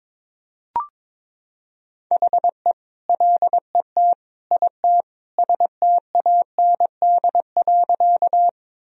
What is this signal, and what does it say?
Morse code, heard as a keyed tone, about 22 wpm, 700 Hz. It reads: HE LET IT STAND.
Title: Sentences from Top 1000 Words - Review Part 1 - 22wpm